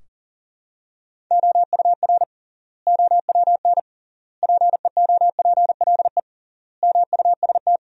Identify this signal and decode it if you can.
Morse code, heard as a keyed tone, about 40 wpm, 700 Hz. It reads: OUR OWN PEOPLE MUST
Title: 3x Word-Spacing - Sentences from Top 300 Words - 40wpm